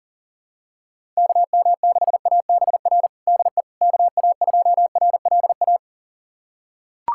Morse code, heard as a keyed tone, about 40 wpm, 700 Hz. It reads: KM6ABR DE KA2RLA
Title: USA Callsign DE Callsign 40wpm